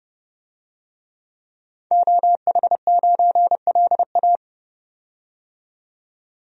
Morse code, heard as a keyed tone, about 29 wpm, 700 Hz. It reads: OH9LA
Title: DXCC-100 Call Signs - 5 Characters - 30wpm